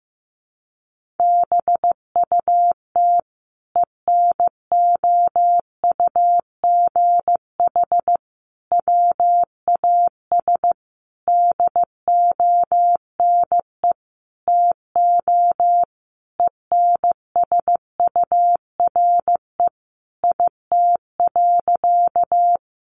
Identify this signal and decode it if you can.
Morse code, heard as a keyed tone, about 15 wpm, 700 Hz. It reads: BUT ENOUGH WAS DONE TO ENSURE IT.
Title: Sentences from Top 1100 Words 15wpm